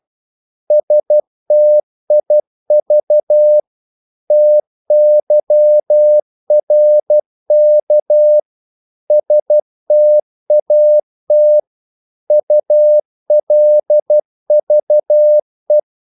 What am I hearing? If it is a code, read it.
Morse code, heard as a keyed tone, about 12 wpm, 600 Hz. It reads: STIV TYRK STAT ULVE
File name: Korte ord DK 12wpm